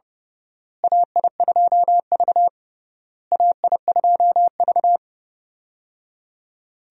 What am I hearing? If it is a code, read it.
Morse code, heard as a keyed tone, about 30 wpm, 700 Hz. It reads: AI2V AI2V
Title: USA Call Signs - 4 Characters - Repeated 3x - 30wpm